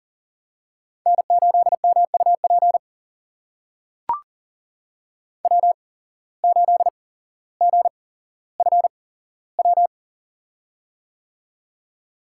Morse code, heard as a keyed tone, about 40 wpm, spaced as for 17 wpm, 700 Hz. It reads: N8MUP W8GFW
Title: USA Call Signs - 5 Characters - 8x Character Spacing - 40wpm